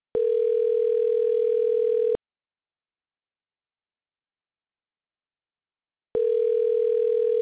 Ringback tone (North America), 440 + 480 Hz.